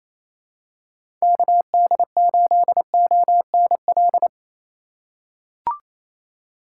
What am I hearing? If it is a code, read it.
Morse code, heard as a keyed tone, about 27 wpm, 700 Hz. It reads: KD8ONL
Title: USA Call Signs - 4x Character Spacing - 28wpm